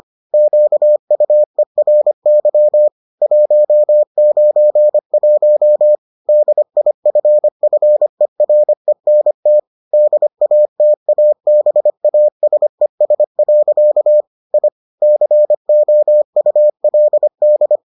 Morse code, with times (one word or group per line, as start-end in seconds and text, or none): 0.3-2.9 QUERY
3.2-6.0 191
6.3-9.6 DIFFERENT
9.9-14.2 DATABASES.
14.5-14.7 I
15.0-17.8 COULD